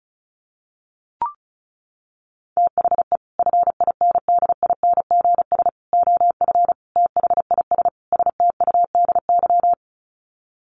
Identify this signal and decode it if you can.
Morse code, heard as a keyed tone, about 34 wpm, 700 Hz. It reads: THE FINDINGS OF THIS STUDY
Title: Sets of 5 Words Encore2 35wpm